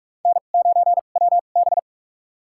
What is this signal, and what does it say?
Morse code, heard as a keyed tone, about 46 wpm, spaced as for 33 wpm, 700 Hz. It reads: N9WB
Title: USA Call Signs - 4 Characters - 2x Character Spacing - 45wpm